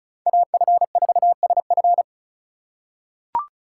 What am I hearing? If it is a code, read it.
Morse code, heard as a keyed tone, about 34 wpm, 700 Hz. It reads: AF4SF
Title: USA Call Signs - 5 Characters - 2x Character Spacing - 35wpm